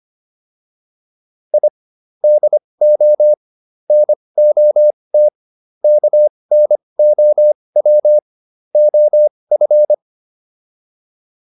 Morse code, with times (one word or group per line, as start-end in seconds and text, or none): 1.5-1.7 I
2.2-3.3 DO
3.9-5.3 NOT
5.8-8.2 KNOW
8.8-9.9 OF